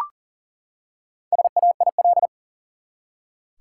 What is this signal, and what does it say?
Morse code, heard as a keyed tone, about 40 wpm, 700 Hz. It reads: SAIL
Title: Four Letter Words - 40wpm